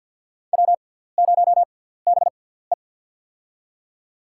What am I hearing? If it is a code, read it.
Morse code, heard as a keyed tone, about 48 wpm, spaced as for 17 wpm, 700 Hz. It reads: W0BE
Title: USA Call Signs - 4 Characters - 6x Character Spacing - 50wpm